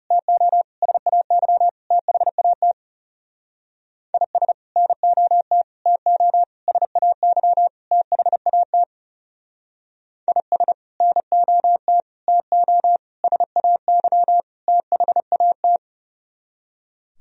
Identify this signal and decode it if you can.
Morse code, heard as a keyed tone, about 37 wpm, 700 Hz. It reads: TO SAY THAT IS NOT TO SAY THAT IS NOT TO SAY THAT
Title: Speed-Racing - Sets of 5 Words 30wpm